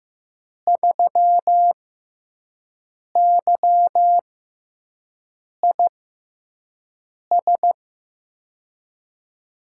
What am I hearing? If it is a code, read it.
Morse code, heard as a keyed tone, about 15 wpm, spaced as for 5 wpm, 700 Hz. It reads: 3YIS
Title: USA Call Signs - 6 Characters - 6x Character Spacing - 15wpm